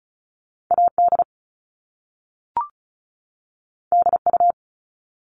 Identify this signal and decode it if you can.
Morse code, heard as a keyed tone, about 36 wpm, 700 Hz. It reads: AD DU
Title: Top Two Letter Combinations in Words 35wpm